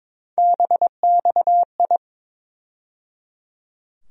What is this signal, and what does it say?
Morse code, heard as a keyed tone, about 21 wpm, 700 Hz. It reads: BXI
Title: USA Call Sign Postfixes - 3 Letters 22wpm